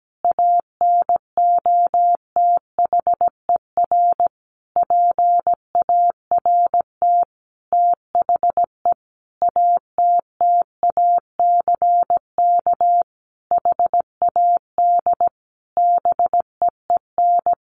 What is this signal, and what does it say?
Morse code, heard as a keyed tone, about 17 wpm, 700 Hz. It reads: ANOTHER PART THE ATTACK HAD BEEN
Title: Sentences from Top 1500 Words 17wpm